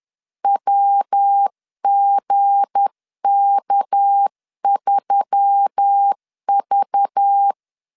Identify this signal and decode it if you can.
Morse code, heard as a keyed tone, about 10 wpm, 800 Hz. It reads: WGK3V
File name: LEZIONI / 7 GF9Z / 7 G 10wpm 2501